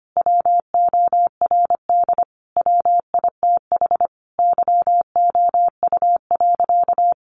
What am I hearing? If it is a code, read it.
Morse code, heard as a keyed tone, about 25 wpm, 700 Hz. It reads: WORD WITH YOU.
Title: Sentences from Top 300 Words 25wpm